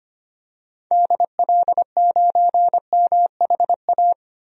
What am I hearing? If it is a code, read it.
Morse code, heard as a keyed tone, about 25 wpm, 690 Hz. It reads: DL9MHA
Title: DXCC-100 Call Signs - 6 Characters - 25wpm